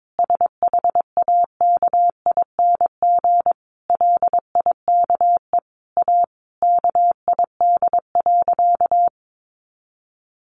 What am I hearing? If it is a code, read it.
Morse code, heard as a keyed tone, about 22 wpm, 700 Hz. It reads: SHAKING LIKE A KID.
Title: A Sound Of Thunder 22wpm